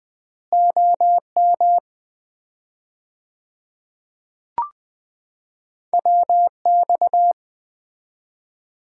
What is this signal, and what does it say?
Morse code, heard as a keyed tone, about 20 wpm, 700 Hz. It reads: OM WX
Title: Common QSO Phrases - 4x Word-Spacing - 20wpm